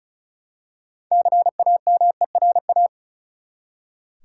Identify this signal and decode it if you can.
Morse code, heard as a keyed tone, about 34 wpm, 700 Hz. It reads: CAMERA
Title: Six Letter Words - 35wpm